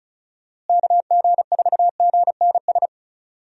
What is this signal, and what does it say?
Morse code, heard as a keyed tone, about 34 wpm, 700 Hz. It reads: KG4GNS
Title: USA Call Signs - 6 Characters - Repeated 2x - 35wpm